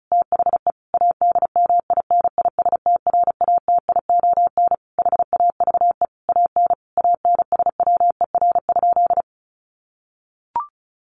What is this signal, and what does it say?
Morse code, heard as a keyed tone, about 34 wpm, 700 Hz. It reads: THE ADMINISTRATION HAVE AN ANSWER?
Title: Sentences from Top 1300 Words 35wpm